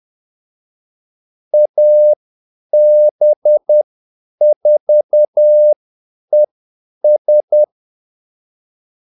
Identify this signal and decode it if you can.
Morse code, heard as a keyed tone, about 10 wpm, spaced as for 8 wpm, 600 Hz. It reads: AB4ES
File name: Callsigns-10-8wpm